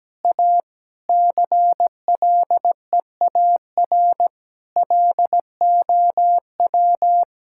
Morse code, heard as a keyed tone, about 17 wpm, 700 Hz. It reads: A CLEAR LOW
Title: Sentences from Top 1600 Words 17wpm